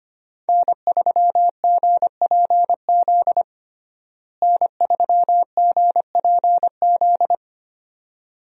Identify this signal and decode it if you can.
Morse code, heard as a keyed tone, about 25 wpm, 700 Hz. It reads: N3GPZ N3GPZ
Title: USA Call Signs - 5 Characters - Repeated 2x - 25wpm